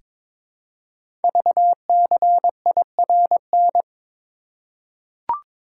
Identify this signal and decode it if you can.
Morse code, heard as a keyed tone, about 22 wpm, 700 Hz. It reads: VCIRN
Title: Code Groups - 5 Letters - 22wpm